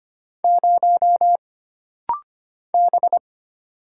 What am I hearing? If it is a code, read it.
Morse code, heard as a keyed tone, about 25 wpm, 700 Hz. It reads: 0 B
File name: Single Character (Letter:Number:Punctuation:Prosign) - Rapid-Fire - 25wpm